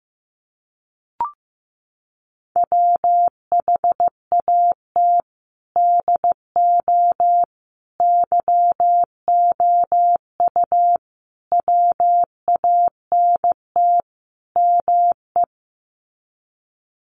Morse code, heard as a keyed tone, about 15 wpm, 700 Hz. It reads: WHAT DO YOU WANT ME
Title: Sets of 5 Words Encore2 15wpm